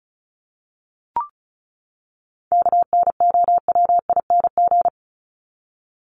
Morse code, heard as a keyed tone, about 34 wpm, 700 Hz. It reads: KNOWING
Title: ING Suffix - Top 100 Words with Base Words 35wpm